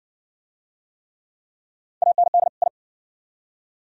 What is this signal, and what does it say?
Morse code, heard as a keyed tone, about 56 wpm, 700 Hz. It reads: AND I